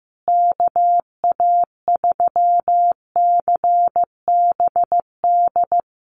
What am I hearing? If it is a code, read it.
Morse code, heard as a keyed tone, about 15 wpm, 700 Hz. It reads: KA3CBD
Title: USA Call Signs Repeated 3x 15wpm